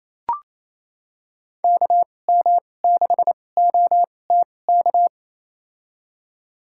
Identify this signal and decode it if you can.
Morse code, heard as a keyed tone, about 28 wpm, spaced as for 20 wpm, 700 Hz. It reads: KM6OTK
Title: USA Call Signs - 2x Character Spacing - 28wpm